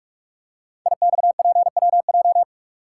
Morse code, heard as a keyed tone, about 44 wpm, 700 Hz. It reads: IXPWJ
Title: Code Groups - 5 Letters - 45wpm